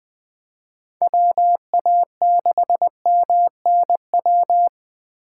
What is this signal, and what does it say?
Morse code, heard as a keyed tone, about 20 wpm, 700 Hz. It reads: WA6MNW
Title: USA Call Signs - 6 Characters - 4x Character Spacing - 20wpm